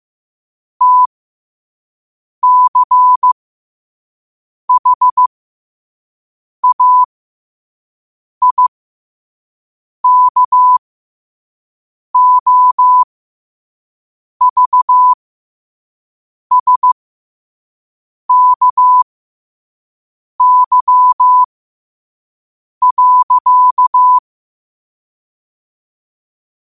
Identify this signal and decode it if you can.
Morse code, heard as a keyed tone, about 15 wpm, spaced as for 5 wpm, 1000 Hz. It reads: TCHAIKOVSKY.
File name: qotdfiles / 5wpm